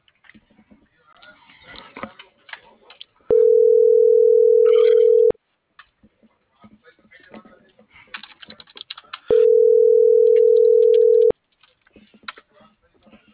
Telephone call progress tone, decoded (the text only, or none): ringback tone